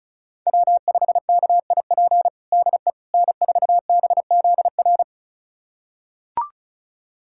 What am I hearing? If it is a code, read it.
Morse code, heard as a keyed tone, about 34 wpm, 700 Hz. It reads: W5KIP DE N4BZR
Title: USA Callsign DE Callsign - 5 Characters - 35wpm